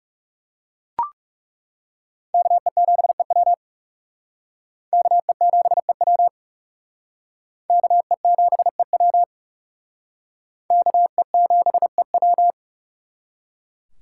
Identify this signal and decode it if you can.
Morse code, heard as a keyed tone, about 36 wpm, 700 Hz. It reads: KE7EW KE7EW KE7EW KE7EW
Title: Speed Racing - USA 2x2 Call Signs 30wpm